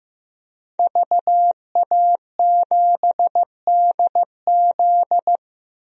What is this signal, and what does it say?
Morse code, heard as a keyed tone, about 15 wpm, 700 Hz. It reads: VA7DZ
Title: DXCC-100 Call Signs - 5 Characters - 15wpm